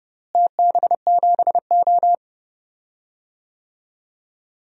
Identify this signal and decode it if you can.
Morse code, heard as a keyed tone, about 30 wpm, 700 Hz. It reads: TB7O
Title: DXCC-100 Call Signs - 4 Characters - Repeated 3x - 30wpm